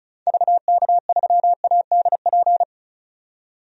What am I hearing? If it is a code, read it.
Morse code, heard as a keyed tone, about 34 wpm, 700 Hz. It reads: VK3ADP
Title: DXCC-100 Call Signs - 6 Characters - 35wpm